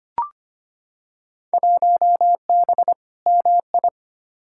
Morse code, heard as a keyed tone, about 25 wpm, 700 Hz. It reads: 1B MI
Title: Field Day Exchanges - 25wpm